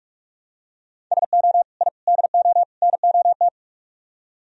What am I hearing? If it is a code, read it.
Morse code, heard as a keyed tone, about 44 wpm, 700 Hz. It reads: SO I DO NOT